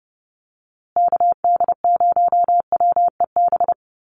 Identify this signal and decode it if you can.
Morse code, heard as a keyed tone, about 30 wpm, 700 Hz. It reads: KD0WEB
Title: USA Call Signs Repeated 2x 30wpm